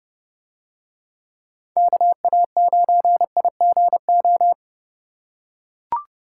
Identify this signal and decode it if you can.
Morse code, heard as a keyed tone, about 30 wpm, 700 Hz. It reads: KA9IGO